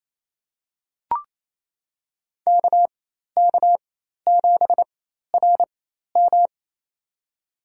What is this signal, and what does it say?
Morse code, heard as a keyed tone, about 28 wpm, spaced as for 13 wpm, 700 Hz. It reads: KK7RM
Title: USA Call Signs - 5 Characters - 4x Character Spacing - 28wpm